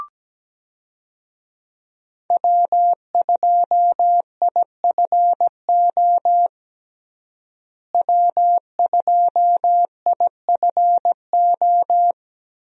Morse code, heard as a keyed tone, about 17 wpm, 700 Hz. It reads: W2IFO W2IFO
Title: USA Call Signs - 5 Characters - Repeated 2x - 17wpm